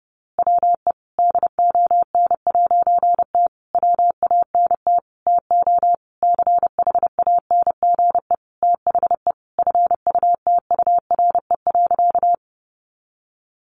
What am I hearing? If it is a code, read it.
Morse code, heard as a keyed tone, about 30 wpm, 700 Hz. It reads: WE DON'T WANT TO CHANGE THE FUTURE.
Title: A Sound Of Thunder 30wpm